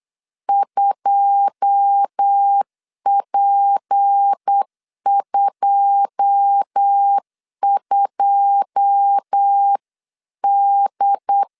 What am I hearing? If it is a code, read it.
Morse code, heard as a keyed tone, about 8 wpm, 800 Hz. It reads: 2P22D
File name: LEZIONI / 9 2 8wpm 3437